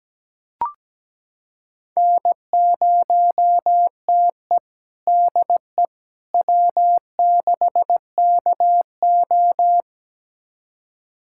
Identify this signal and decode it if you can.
Morse code, heard as a keyed tone, about 17 wpm, 700 Hz. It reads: N0TE DE W6KO
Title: USA Callsign DE Callsign - 4 Characters - 17wpm